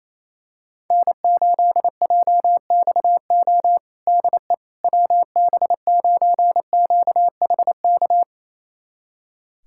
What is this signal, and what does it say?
Morse code, heard as a keyed tone, about 28 wpm, 700 Hz. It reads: N8JXO DE WB9QHK
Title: USA Callsign DE Callsign 28wpm